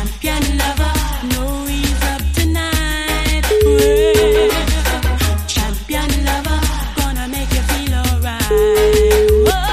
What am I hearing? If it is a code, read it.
Ringback tone (Europe, 425 Hz), with noise.